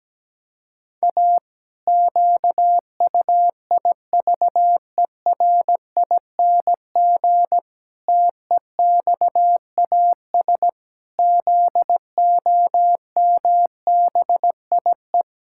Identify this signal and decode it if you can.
Morse code, heard as a keyed tone, about 17 wpm, 700 Hz. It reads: A QUIVERING TEXAS ZOMBIE
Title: Pangram Challenge v2 - 17wpm